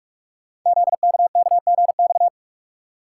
Morse code, heard as a keyed tone, about 44 wpm, 700 Hz. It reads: ZKKGX
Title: Code Groups - 5 Letter:Numbers - 45wpm